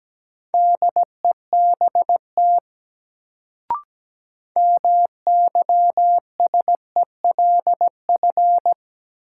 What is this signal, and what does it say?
Morse code, heard as a keyed tone, about 17 wpm, 700 Hz. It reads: DEBT MYSELF
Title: Sentences from Top 1300 Words - New Word Drill - 17wpm